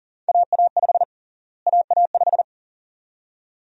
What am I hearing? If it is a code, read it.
Morse code, heard as a keyed tone, about 40 wpm, 700 Hz. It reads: AA5 AA5
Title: USA Call Sign Prefixes - Repeated 2x - 40wpm